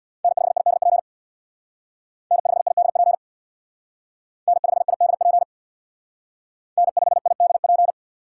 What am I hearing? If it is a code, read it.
Morse code, heard as a keyed tone, about 60 wpm, 700 Hz. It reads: N5IDP N5IDP N5IDP N5IDP
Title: Speed Racing - USA 1x3 Call Signs 50wpm